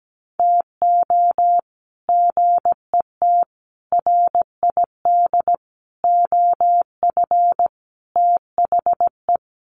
Morse code, heard as a keyed tone, about 17 wpm, 700 Hz. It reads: TO GET RID OF THE
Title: Sets of 5 Words 17wpm